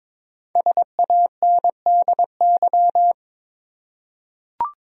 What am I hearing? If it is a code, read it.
Morse code, heard as a keyed tone, about 21 wpm, 700 Hz. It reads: SANDY
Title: USA Common Names - 22wpm